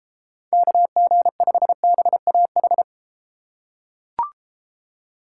Morse code, heard as a keyed tone, about 32 wpm, 700 Hz. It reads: KG5BAH